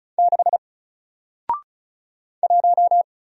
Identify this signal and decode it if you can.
Morse code, heard as a keyed tone, about 35 wpm, 700 Hz. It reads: 6 1
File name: Single-Letter-Number-35wpm